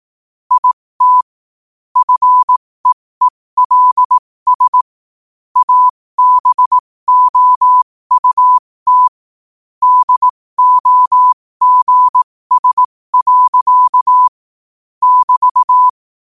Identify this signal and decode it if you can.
Morse code, heard as a keyed tone, about 18 wpm, spaced as for 15 wpm, 1000 Hz. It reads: IT FEELS ABOUT DOGS. <BT>
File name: qotdfiles / 15wpm